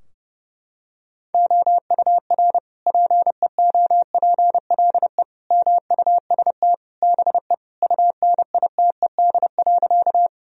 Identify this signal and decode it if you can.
Morse code, heard as a keyed tone, about 31 wpm, 700 Hz. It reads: OUR PEOPLE MUST BE UNITED.
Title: Sentences from Top 1000 Words - Review Part 4 - 30wpm